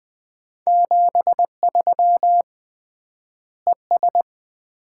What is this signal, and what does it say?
Morse code, heard as a keyed tone, about 20 wpm, 700 Hz. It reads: 73 ES
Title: Common QSO Phrases - 3x Word-Spacing - 20wpm